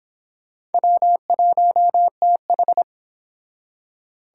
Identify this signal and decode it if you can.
Morse code, heard as a keyed tone, about 29 wpm, 700 Hz. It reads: W1TH